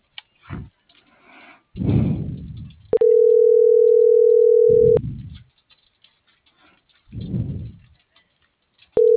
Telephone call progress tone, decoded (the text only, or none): ringback tone